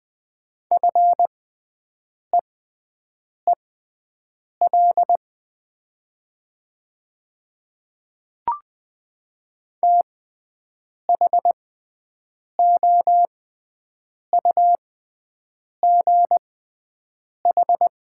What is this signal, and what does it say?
Morse code, heard as a keyed tone, about 20 wpm, spaced as for 7 wpm, 700 Hz. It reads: FEEL THOUGH
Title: 6x Character Spacing - Top 500 Words - 20wpm